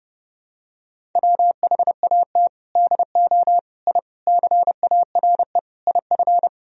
Morse code, heard as a keyed tone, about 30 wpm, 700 Hz. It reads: WHAT DO I CARE IF
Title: Sentences from Top 1000 Words - Review Part 1 - 30wpm